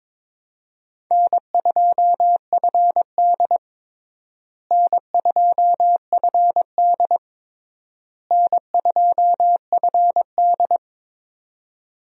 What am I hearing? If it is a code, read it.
Morse code, heard as a keyed tone, about 22 wpm, 700 Hz. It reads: N2FD N2FD N2FD
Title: USA Call Signs - 4 Characters - Repeated 3x - 22wpm